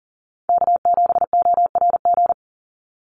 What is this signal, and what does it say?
Morse code, heard as a keyed tone, about 38 wpm, 700 Hz. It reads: K7ORG
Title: NA SOTA Chaser Callsigns 40wpm